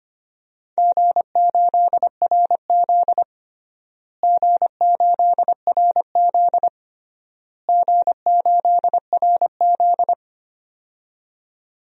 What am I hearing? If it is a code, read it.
Morse code, heard as a keyed tone, about 25 wpm, 700 Hz. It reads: G8RZ G8RZ G8RZ
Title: DXCC-100 Call Signs - 4 Characters - Repeated 3x - 25wpm